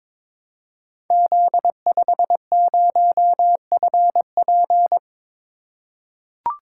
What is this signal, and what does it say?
Morse code, heard as a keyed tone, about 22 wpm, 700 Hz. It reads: Z50FP
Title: Code Groups - 5 Letter:Numbers - 22wpm